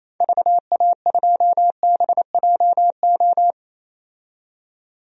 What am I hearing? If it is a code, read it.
Morse code, heard as a keyed tone, about 28 wpm, 700 Hz. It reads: VA2BJO
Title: DXCC-100 Call Signs - 6 Characters - 28wpm